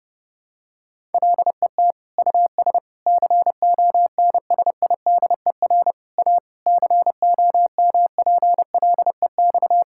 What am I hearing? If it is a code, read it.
Morse code, heard as a keyed tone, about 30 wpm, 700 Hz. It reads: LET US CONSIDER A COMPLEX
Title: Sentences from Top 1000 Words - Review Part 6 - 30wpm